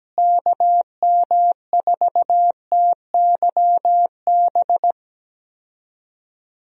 Morse code, heard as a keyed tone, about 17 wpm, 700 Hz. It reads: KM4TYB
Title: USA Call Signs - 6 Characters - Repeated 2x - 17wpm